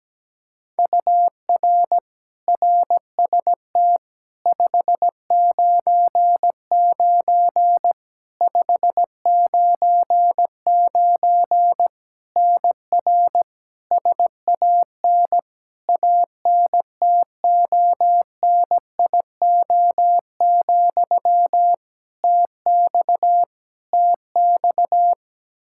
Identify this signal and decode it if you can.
Morse code, heard as a keyed tone, about 17 wpm, 700 Hz. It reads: UR RST 599 599 NR SAN ANTONIO, TX TX
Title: K1USN SST QSO Practice 17wpm